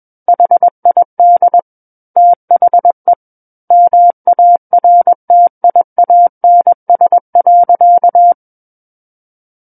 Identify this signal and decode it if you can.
Morse code, heard as a keyed tone, about 21 wpm, 700 Hz. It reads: HID THE MARTIANS.
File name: Book One / 12-Chapter 12 - 21 WPM